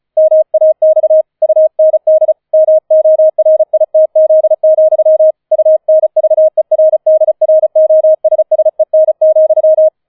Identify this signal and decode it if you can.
Morse code, heard as a keyed tone, about 36 wpm, 630 Hz. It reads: MAX UND MORITZ, UNVERDROSSEN,
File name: 004 Vierter Streich 35wpm